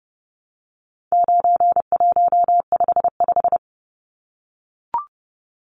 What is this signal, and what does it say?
Morse code, heard as a keyed tone, about 30 wpm, 700 Hz. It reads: 9155
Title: Numbers - 4-Digit 30wpm